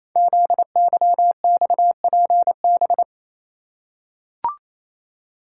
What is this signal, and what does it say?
Morse code, heard as a keyed tone, about 27 wpm, 700 Hz. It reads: ZYXPB
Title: Code Groups - 5 Letter:Numbers - 28wpm